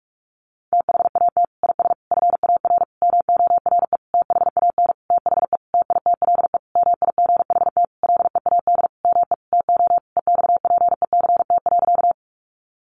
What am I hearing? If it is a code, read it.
Morse code, heard as a keyed tone, about 44 wpm, 700 Hz. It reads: THAT IS FAR MORE THAN THE TITLE MIGHT LEAD ME TO EXPECT.
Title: Sentences from Top 1200 Words 45wpm